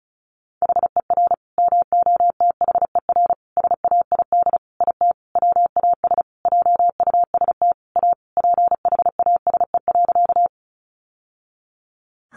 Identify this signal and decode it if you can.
Morse code, heard as a keyed tone, about 34 wpm, 700 Hz. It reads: HER MOTHER SAID IT WAS JUST A PHASE.
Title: Sentences from Top 1200 Words 35wpm